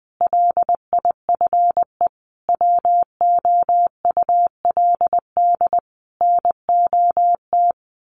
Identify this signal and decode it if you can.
Morse code, heard as a keyed tone, about 20 wpm, 700 Hz. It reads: LIFE WOULD NOT
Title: Sentences from Top 1300 Words 20wpm